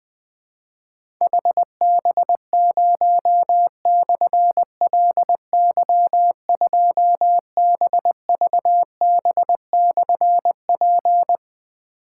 Morse code, heard as a keyed tone, about 20 wpm, 700 Hz. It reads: HB0/LY2BVB/P
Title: Difficult Call Signs 20wpm